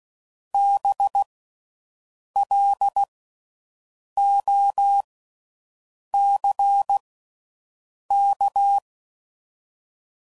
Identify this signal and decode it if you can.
Morse code, heard as a keyed tone, about 16 wpm, 800 Hz. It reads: B L O C K
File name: Suffixes / General / Houses